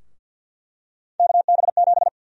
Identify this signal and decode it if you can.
Morse code, heard as a keyed tone, about 51 wpm, 700 Hz. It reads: KB7
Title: USA Call Sign Prefixes - Repeated 2x - 50wpm